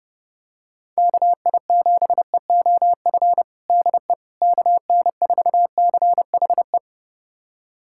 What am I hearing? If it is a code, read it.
Morse code, heard as a keyed tone, about 30 wpm, 700 Hz. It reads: KI7EOF DE KN4CHE
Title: USA Callsign DE Callsign - 6 Characters - 30wpm